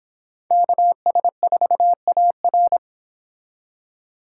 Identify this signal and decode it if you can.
Morse code, heard as a keyed tone, about 30 wpm, 700 Hz. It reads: KS4AR